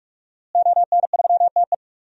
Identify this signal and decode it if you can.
Morse code, heard as a keyed tone, about 46 wpm, 700 Hz. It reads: ON3TE
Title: DXCC-100 Call Signs - 5 Characters - Repeated 2x - 45wpm